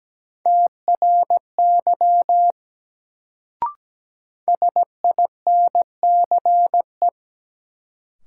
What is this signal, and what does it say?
Morse code, heard as a keyed tone, about 17 wpm, 700 Hz. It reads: TRY SINCE
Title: Sets of 2 Words - New Word Drill - 17wpm